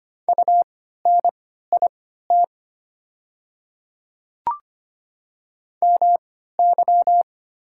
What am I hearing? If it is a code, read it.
Morse code, heard as a keyed tone, about 25 wpm, spaced as for 14 wpm, 690 Hz. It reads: UNIT MY
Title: 3x Character Spacing - Top 500 Words - 25wpm